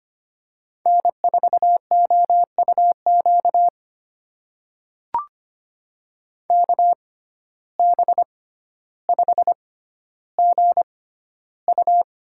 Morse code, heard as a keyed tone, about 25 wpm, spaced as for 13 wpm, 700 Hz. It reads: N4OUQ KB5GU
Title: USA Call Signs - 5 Characters - 6x Character Spacing - 25wpm